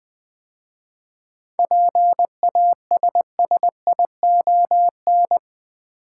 Morse code, heard as a keyed tone, about 20 wpm, 700 Hz. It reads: PASSION
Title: 7 Letter Words - First 3 Letters Given - 20wpm